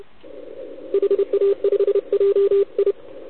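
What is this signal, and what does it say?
Morse code, heard as a keyed tone, about 29 wpm, 400 Hz, with radio noise. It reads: HA5JI
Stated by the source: CW from the OKOM DX contest on 80 m, received with a TS480.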